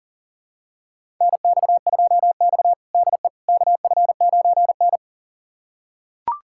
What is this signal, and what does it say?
Morse code, heard as a keyed tone, about 40 wpm, 700 Hz. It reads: NX2X DE KF9N
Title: USA Callsign DE Callsign - 4 Characters - 40wpm